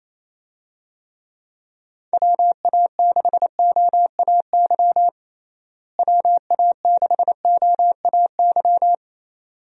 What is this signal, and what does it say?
Morse code, heard as a keyed tone, about 28 wpm, 700 Hz. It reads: WA6OAY WA6OAY
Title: USA Call Signs - 6 Characters - Repeated 2x - 28wpm